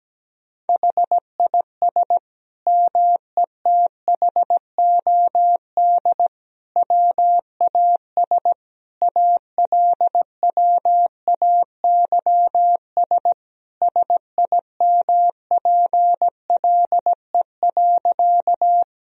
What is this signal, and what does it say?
Morse code, heard as a keyed tone, about 17 wpm, 700 Hz. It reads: HIS METHOD WAS ALWAYS SIMPLE.
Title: Sentences from Top 1000 Words - Review Part 3 - 17wpm